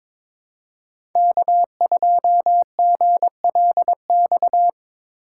Morse code, heard as a keyed tone, about 22 wpm, 700 Hz. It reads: K2GLX
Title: USA Call Signs - 6x Character Spacing - 22wpm